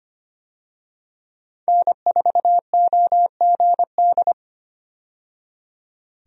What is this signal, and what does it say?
Morse code, heard as a keyed tone, about 25 wpm, 700 Hz. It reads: N4OGD
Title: USA Call Signs - 5 Characters - 25wpm